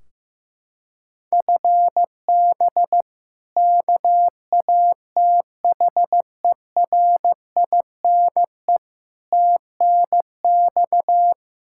Morse code, heard as a keyed tone, about 15 wpm, 700 Hz. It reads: FB KATHERINE TNX
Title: Practice QSO - Exchange 3 - More Info - 15wpm